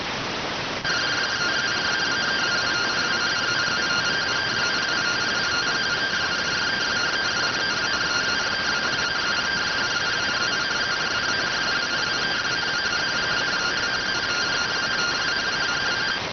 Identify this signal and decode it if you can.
Two-tone teleprinter signal, about 170 Hz shift.